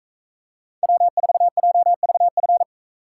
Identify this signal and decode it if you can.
Morse code, heard as a keyed tone, about 43 wpm, 700 Hz. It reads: W4JVF